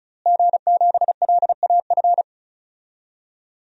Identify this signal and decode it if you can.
Morse code, heard as a keyed tone, about 34 wpm, 700 Hz. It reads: G7LAF
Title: DXCC-100 Call Signs - 5 Characters - Repeated 2x - 35wpm